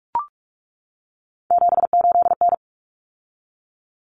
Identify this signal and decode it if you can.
Morse code, heard as a keyed tone, about 44 wpm, 700 Hz. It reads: 78N
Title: Common Cut Numbers - 3-Digit 45wpm